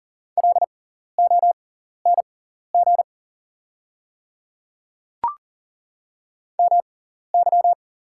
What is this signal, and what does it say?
Morse code, heard as a keyed tone, about 40 wpm, spaced as for 14 wpm, 700 Hz. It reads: LONG MY
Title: 6x Character Spacing - Top 100 Words - 40wpm